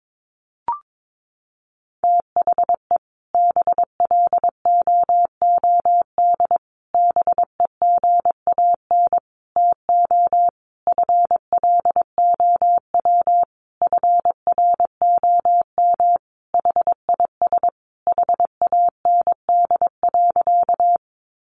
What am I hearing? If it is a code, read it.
Morse code, heard as a keyed tone, about 22 wpm, 700 Hz. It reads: THE BLOOD BEGAN TO FLOW FROM HIS HAND.
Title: Sentences from Top 1200 Words 22wpm